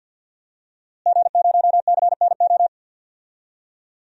Morse code, heard as a keyed tone, about 47 wpm, 700 Hz. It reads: G0CNO
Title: DXCC-100 Call Signs - 5 Characters - Repeated 3x - 50wpm